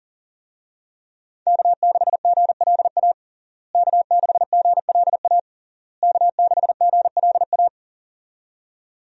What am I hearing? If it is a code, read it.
Morse code, heard as a keyed tone, about 40 wpm, 700 Hz. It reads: K6GLA K6GLA K6GLA
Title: USA Call Signs - 5 Characters - Repeated 3x - 40wpm